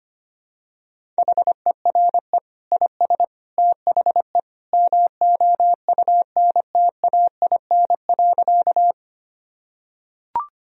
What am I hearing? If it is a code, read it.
Morse code, heard as a keyed tone, about 25 wpm, 700 Hz. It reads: HERE IS THE MOUNTAIN.
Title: Sentences from Top 1000 Words - Review Part 1 - 25wpm